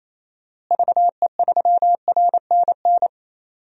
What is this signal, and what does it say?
Morse code, heard as a keyed tone, about 27 wpm, 700 Hz. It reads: VE3RNN
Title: DXCC-100 Call Signs - 6 Characters - Repeated 2x - 28wpm